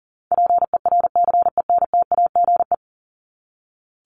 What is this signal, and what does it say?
Morse code, heard as a keyed tone, about 40 wpm, 700 Hz. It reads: PERCENTAGE
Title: Top 1001-1250 Words 40wpm